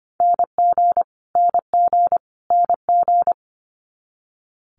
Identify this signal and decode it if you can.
Morse code, heard as a keyed tone, about 25 wpm, 700 Hz. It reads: NG NG NG
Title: Multi-letter Phonemes Repeated 3x 25wpm